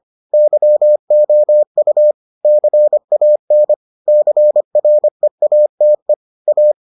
Morse code, heard as a keyed tone, about 25 wpm, 600 Hz. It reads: YOU CAN CREATE A